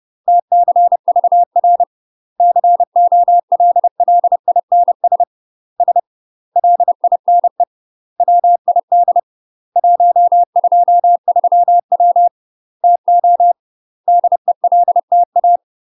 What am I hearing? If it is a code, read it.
Morse code, heard as a keyed tone, about 30 wpm, 700 Hz. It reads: TCVR COLLINS S LINE WID 123W TO DELTA
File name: UK MASTER QSOs TRAINING / 30WPM / UK MASTER QSOs 0001